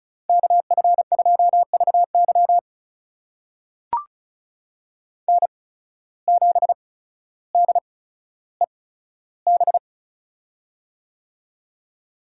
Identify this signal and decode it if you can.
Morse code, heard as a keyed tone, about 36 wpm, spaced as for 15 wpm, 700 Hz. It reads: KF2VY N7DEB
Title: USA Call Signs - 5 Characters - 8x Character Spacing - 35wpm